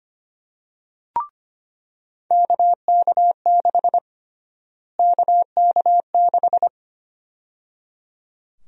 Morse code, heard as a keyed tone, about 25 wpm, 700 Hz. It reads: KK6 KK6
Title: USA Call Sign Prefixes - Repeated 2x - 25wpm